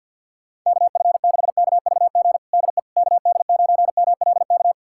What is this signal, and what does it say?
Morse code, heard as a keyed tone, about 53 wpm, 700 Hz. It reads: KF6KVG DE KD9MLK
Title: USA Callsign DE Callsign 50wpm